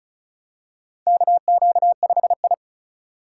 Morse code, heard as a keyed tone, about 36 wpm, 700 Hz. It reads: KQ5I
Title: USA Call Signs - 3x Character Spacing - 35wpm